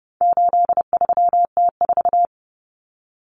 Morse code, heard as a keyed tone, about 30 wpm, 700 Hz. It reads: 83T4
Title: Common Cut Numbers - 4-Digit 30wpm